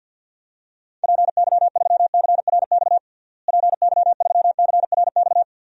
Morse code, heard as a keyed tone, about 46 wpm, 700 Hz. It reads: PY3CRX PY3CRX
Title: DXCC-100 Call Signs - 6 Characters - Repeated 3x - 50wpm